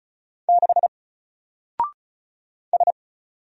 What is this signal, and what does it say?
Morse code, heard as a keyed tone, about 36 wpm, 700 Hz. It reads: 6 S
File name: Single-Letter-Number-35wpm